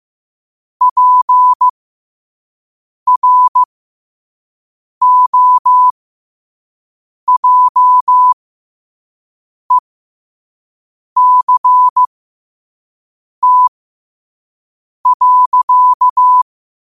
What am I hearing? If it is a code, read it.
Morse code, heard as a keyed tone, about 15 wpm, spaced as for 5 wpm, 1000 Hz. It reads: PROJECT.